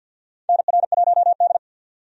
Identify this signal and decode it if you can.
Morse code, heard as a keyed tone, about 51 wpm, 700 Hz. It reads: NR1D
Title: USA Call Signs - 4 Characters - 50wpm